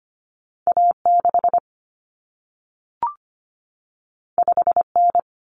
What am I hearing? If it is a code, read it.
Morse code, heard as a keyed tone, about 25 wpm, 700 Hz. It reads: A6 5N
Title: Common Cut Numbers - 2-Digit 25wpm